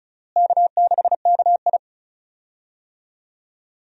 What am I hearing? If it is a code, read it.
Morse code, heard as a keyed tone, about 36 wpm, 700 Hz. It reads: K6KI
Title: USA Call Signs - 4 Characters - Repeated 3x - 35wpm